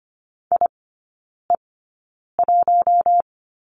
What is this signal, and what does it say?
Morse code, heard as a keyed tone, about 25 wpm, spaced as for 9 wpm, 700 Hz. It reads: IE1
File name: course / 25:10wpm / 093 Lesson 16e - Letter c - Three Characters 25wpm